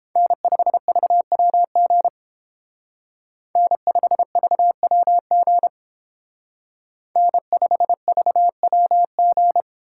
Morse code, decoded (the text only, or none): N5VWG N5VWG N5VWG